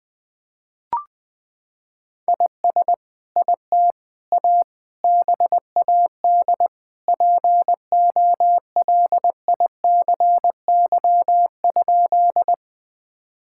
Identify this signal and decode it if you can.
Morse code, heard as a keyed tone, about 20 wpm, 700 Hz. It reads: IS IT A BAD POLICY?
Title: Sentences from Top 1000 Words - Review Part 5 - 20wpm